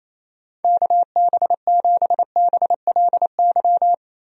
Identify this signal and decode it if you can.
Morse code, heard as a keyed tone, about 28 wpm, 700 Hz. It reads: KB7BLY
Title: USA Call Signs - 6 Characters - Repeated 3x - 28wpm